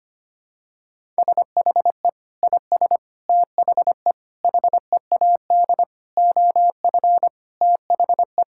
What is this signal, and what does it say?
Morse code, heard as a keyed tone, about 25 wpm, 700 Hz. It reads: SHE IS THE HEAD OF THE
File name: practice / 25wpm / Sentences with 4-Word Phrases - 25wpm